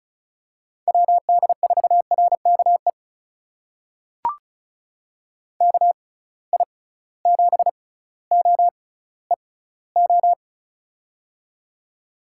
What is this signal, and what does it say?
Morse code, heard as a keyed tone, about 36 wpm, spaced as for 18 wpm, 700 Hz. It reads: WD4RKE KI7OEO
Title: USA Call Signs - 6 Characters - 6x Character Spacing - 35wpm